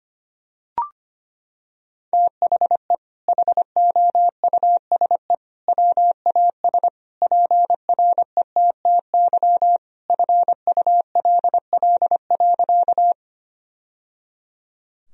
Morse code, heard as a keyed tone, about 25 wpm, 700 Hz. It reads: THE HOUSE WAS PRETTY FULL.
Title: Sentences from Top 1000 Words - Review Part 3 - 25wpm